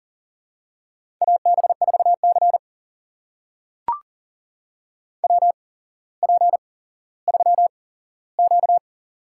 Morse code, heard as a keyed tone, about 40 wpm, spaced as for 17 wpm, 700 Hz. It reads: AB4C WP3Q
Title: USA Call Signs - 4 Characters - 8x Character Spacing - 40wpm